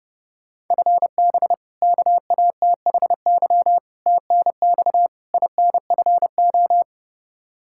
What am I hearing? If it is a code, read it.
Morse code, heard as a keyed tone, about 32 wpm, 700 Hz. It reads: FB KATHY TNX INFO
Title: Practice QSO - Exchange 3 - More Info - 30wpm